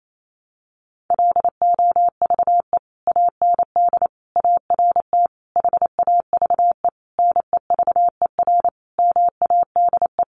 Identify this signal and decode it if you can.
Morse code, heard as a keyed tone, about 28 wpm, 700 Hz. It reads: LOVE AND ART HAVE NEVER MADE
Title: Sentences from Top 1100 Words 28wpm